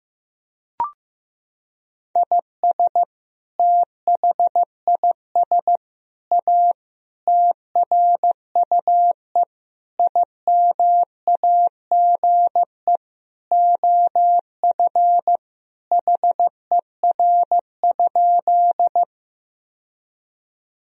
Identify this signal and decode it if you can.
Morse code, heard as a keyed tone, about 16 wpm, 700 Hz. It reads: IS THIS A TRUE IMAGE OF HER?
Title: Sentences from Top 1000 Words - Review Part 4 - 15wpm